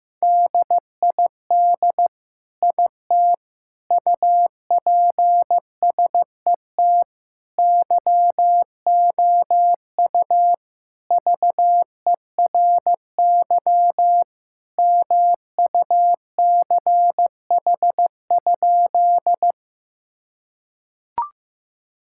Morse code, heard as a keyed tone, about 16 wpm, 700 Hz. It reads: DID IT UPSET YOU VERY MUCH?
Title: Sentences from Top 1000 Words - Review Part 5 - 15wpm